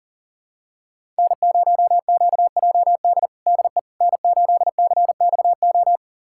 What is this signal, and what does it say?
Morse code, heard as a keyed tone, about 40 wpm, 700 Hz. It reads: N0QJD DE N8CXO
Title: USA Callsign DE Callsign - 5 Characters - 40wpm